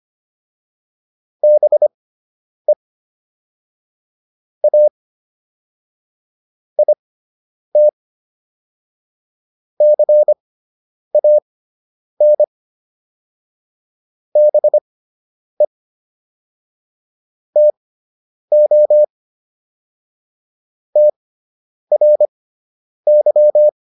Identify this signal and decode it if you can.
Morse code, heard as a keyed tone, about 25 wpm, spaced as for 9 wpm, 600 Hz. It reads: BE A IT CAN BE TO TRY